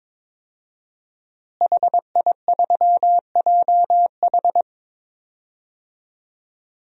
Morse code, heard as a keyed tone, about 21 wpm, 700 Hz. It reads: HI3JH
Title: DXCC-100 Call Signs - 5 Characters - Repeated 2x - 22wpm